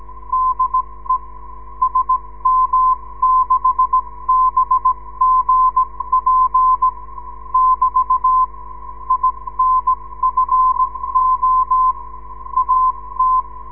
Morse code, heard as a keyed tone, about 17 wpm, spaced as for 15 wpm, 1030 Hz, with radio noise. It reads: DE SM6BGP <BT> INFO AT
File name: ndb zvok / sm6bgp